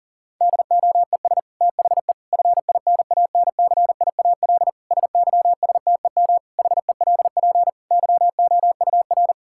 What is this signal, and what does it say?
Morse code, heard as a keyed tone, about 40 wpm, 700 Hz. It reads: DOES THE FINANCIAL SYSTEM HELP YOUR
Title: Sentences from Top 1000 Words - Review Part 5 - 40wpm